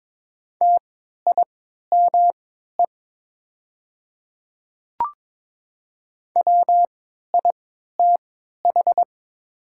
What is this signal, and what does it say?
Morse code, heard as a keyed tone, about 22 wpm, spaced as for 12 wpm, 700 Hz. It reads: TIME WITH
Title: 3x Character Spacing - Top 100 Words - 22wpm